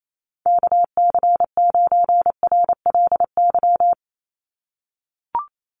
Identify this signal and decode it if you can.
Morse code, heard as a keyed tone, about 28 wpm, 700 Hz. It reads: KC9RLY
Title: USA Call Signs 28wpm